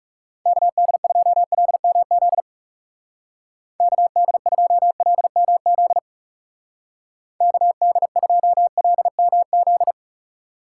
Morse code, heard as a keyed tone, about 35 wpm, 700 Hz. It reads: KD2LMZ KD2LMZ KD2LMZ
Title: Speed Racing - USA Call Signs 30wpm